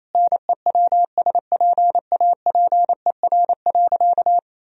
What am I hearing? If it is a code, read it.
Morse code, heard as a keyed tone, about 27 wpm, 700 Hz. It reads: NEWSPAPER.
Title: Sentences from Top 1000 Words - Review Part 4 - 28wpm